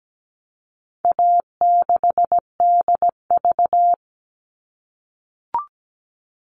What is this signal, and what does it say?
Morse code, heard as a keyed tone, about 17 wpm, 700 Hz. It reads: A6DV
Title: All Cut Numbers - 4-Digit 17wpm